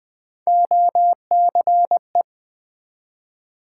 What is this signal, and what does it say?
Morse code, heard as a keyed tone, about 20 wpm, 700 Hz. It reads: OCE
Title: Build A Word - 1 Letter At A Time - 3 to 5-Letter Words - 20wpm